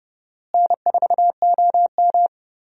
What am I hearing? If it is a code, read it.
Morse code, heard as a keyed tone, about 29 wpm, 700 Hz. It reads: N4OM
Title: USA Call Signs - 4 Characters - Repeated 2x - 30wpm